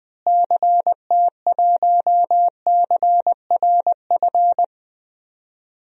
Morse code, heard as a keyed tone, about 20 wpm, 700 Hz. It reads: CT1CRF
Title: DXCC-100 Call Signs - 6 Characters - Repeated 2x - 20wpm